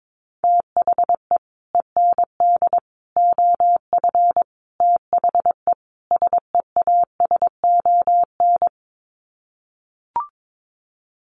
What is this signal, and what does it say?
Morse code, heard as a keyed tone, about 22 wpm, 700 Hz. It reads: THE END OF THE SEASON
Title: Sets of 5 Words Encore 22wpm